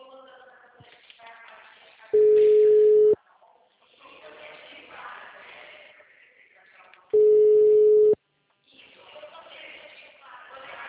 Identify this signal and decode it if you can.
Ringback tone (Europe, 425 Hz), with speech.